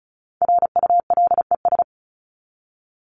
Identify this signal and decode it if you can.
Morse code, heard as a keyed tone, about 34 wpm, 700 Hz. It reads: RULES
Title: ES:S Suffix - Top 100 Words with Base Words 35wpm